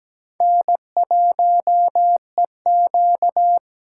Morse code, heard as a keyed tone, about 17 wpm, 700 Hz. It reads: N1EQ
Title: USA Call Signs - 4 Characters - 2x Character Spacing - 17wpm